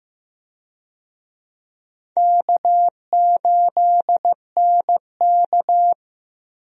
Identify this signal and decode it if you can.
Morse code, heard as a keyed tone, about 15 wpm, 700 Hz. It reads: K8NK
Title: USA Call Signs - 4 Characters - 15wpm